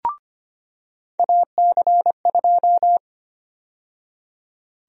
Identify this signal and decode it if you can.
Morse code, heard as a keyed tone, about 25 wpm, 700 Hz. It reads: AC2
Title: USA Call Sign Prefixes - 25wpm